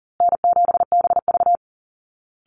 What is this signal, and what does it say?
Morse code, heard as a keyed tone, about 37 wpm, 700 Hz. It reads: N7BV
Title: Easy Call Signs Repeated 3x - 40wpm